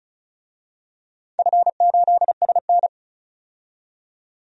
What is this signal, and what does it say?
Morse code, heard as a keyed tone, about 36 wpm, 700 Hz. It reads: F8SN
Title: DXCC-100 Call Signs - 4 Characters - 35wpm